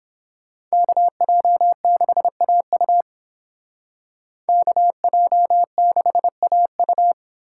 Morse code, heard as a keyed tone, about 31 wpm, 700 Hz. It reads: KJ6AU KJ6AU